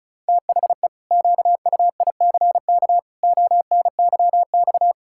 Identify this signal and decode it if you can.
Morse code, heard as a keyed tone, about 36 wpm, 700 Hz. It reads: THE QUICK ONYX
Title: Pangram Challenge v2 - 35wpm